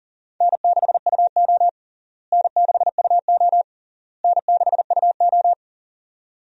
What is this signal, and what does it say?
Morse code, heard as a keyed tone, about 37 wpm, 700 Hz. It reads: N6UO N6UO N6UO
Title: USA Call Signs - 4 Characters - Repeated 3x - 40wpm